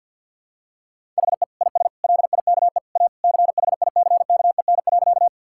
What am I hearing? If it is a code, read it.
Morse code, heard as a keyed tone, about 48 wpm, 700 Hz. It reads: HE IS LIKE A CHICKEN.
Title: Sentences from Top 1000 Words - Review Part 5 - 50wpm